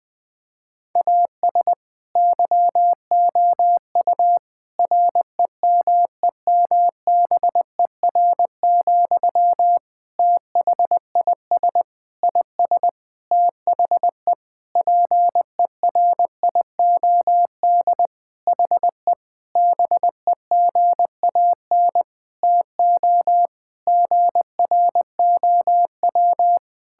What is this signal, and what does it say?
Morse code, heard as a keyed tone, about 21 wpm, 700 Hz. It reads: AS YOU REMEMBER, THIS IS THE PERIOD HE BEGAN TO GROW
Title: Sentences from Top 1000 Words - Review Part 5 - 20wpm